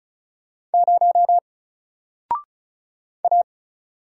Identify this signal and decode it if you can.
Morse code, heard as a keyed tone, about 36 wpm, 700 Hz. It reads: 0 A
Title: Single Character (Letter:Number:Punctuation:Prosign) Alt 35wpm